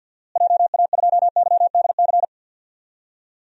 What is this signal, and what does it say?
Morse code, heard as a keyed tone, about 52 wpm, 700 Hz. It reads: JA2YDC
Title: DXCC-100 Call Signs - 6 Characters - Repeated 3x - 50wpm